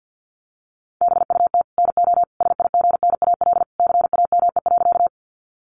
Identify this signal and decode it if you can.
Morse code, heard as a keyed tone, about 50 wpm, 700 Hz. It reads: BUT NO SIGNAL CAME.
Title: Sentences from Top 1400 Words 50wpm